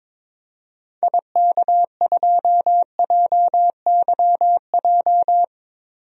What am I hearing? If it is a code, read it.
Morse code, heard as a keyed tone, about 21 wpm, 700 Hz. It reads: IK2JYJ
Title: DXCC-100 Call Signs - 6 Characters - 22wpm